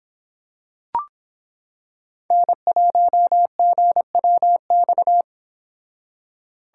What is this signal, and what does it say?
Morse code, heard as a keyed tone, about 26 wpm, 700 Hz. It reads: N1GWX